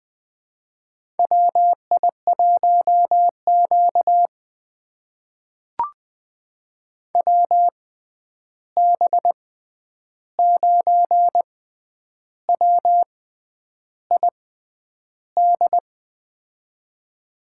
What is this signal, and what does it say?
Morse code, heard as a keyed tone, about 20 wpm, spaced as for 9 wpm, 700 Hz. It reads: WI1Q WB9WID
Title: USA Call Signs - 6x Character Spacing - 20wpm